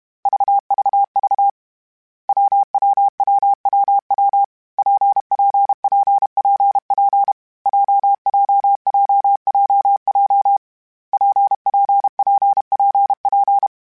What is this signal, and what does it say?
Morse code, heard as a keyed tone, about 32 wpm, 800 Hz. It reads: VVV WWWWW PPPPP JJJJJ PPPPP
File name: Variable Speed / Variable Speed-mp3s / WPJ WPJ1 sp @30